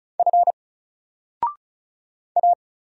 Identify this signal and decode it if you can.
Morse code, heard as a keyed tone, about 34 wpm, 700 Hz. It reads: F A
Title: Single Letter-Number 35wpm